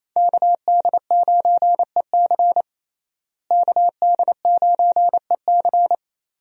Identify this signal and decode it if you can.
Morse code, heard as a keyed tone, about 28 wpm, 700 Hz. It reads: KD9EC KD9EC
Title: USA Call Signs - 5 Characters - Repeated 2x - 28wpm